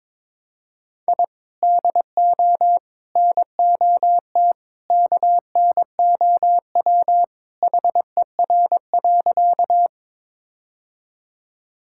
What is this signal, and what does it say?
Morse code, heard as a keyed tone, about 22 wpm, 700 Hz. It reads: I DO NOT KNOW HER.
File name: practice / 22wpm / Sentences with 2-Word Phrases - 22wpm